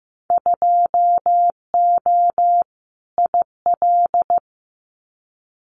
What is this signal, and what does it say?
Morse code, heard as a keyed tone, about 15 wpm, 700 Hz. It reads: 2O IL
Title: Winter Field Day Exchanges 15wpm